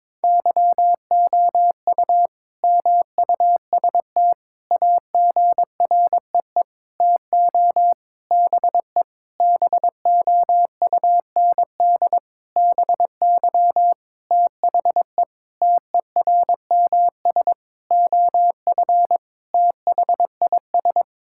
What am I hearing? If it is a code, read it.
Morse code, heard as a keyed tone, about 22 wpm, 700 Hz. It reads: YOU MUST AGREE TO BE BOUND BY THE TERMS OF THIS
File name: Sentences with 9-Word Phrases - 22wpm